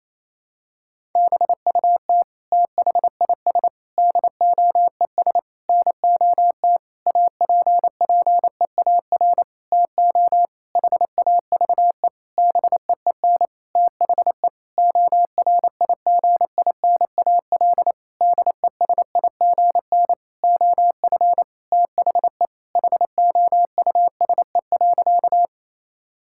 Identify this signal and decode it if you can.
Morse code, heard as a keyed tone, about 28 wpm, 700 Hz. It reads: BUT THIS DOES NOT APPEAR TO HAVE BEEN THE ORIGINAL DESIGN OF THE HOUSE.
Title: Sentences from Top 1000 Words - Review Part 6 - 28wpm